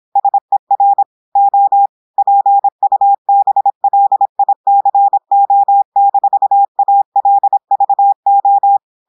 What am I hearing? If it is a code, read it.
Morse code, heard as a keyed tone, about 26 wpm, 800 Hz. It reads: SER O PUBLICO-ALVO
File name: portugais texte en clair 2 vitesse 26 mots minutes